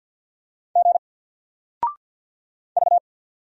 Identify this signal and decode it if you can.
Morse code, heard as a keyed tone, about 50 wpm, 700 Hz. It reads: G V
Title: Single Character (Letter:Number:Punctuation:Prosign) 50wpm